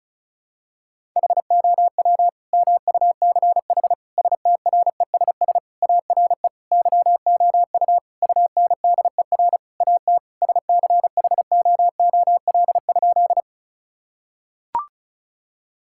Morse code, heard as a keyed tone, about 34 wpm, 700 Hz. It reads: HOW MUCH STRESS ARE YOU UNDER AT SCHOOL?
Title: Sentences from Top 1000 Words - Review Part 6 - 35wpm